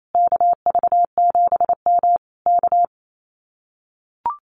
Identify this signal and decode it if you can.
Morse code, heard as a keyed tone, about 28 wpm, 700 Hz. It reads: KV7M K
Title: Common QSO Phrases Encore 28wpm